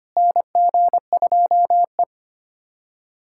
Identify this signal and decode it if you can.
Morse code, heard as a keyed tone, about 25 wpm, 700 Hz. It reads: NG2E
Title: USA Callsign DE Callsign - 4 Characters - 25wpm